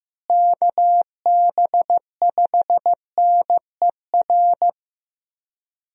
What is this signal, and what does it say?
Morse code, heard as a keyed tone, about 15 wpm, 700 Hz. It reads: KB5NER
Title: USA Call Signs - 6 Characters - Repeated 2x - 15wpm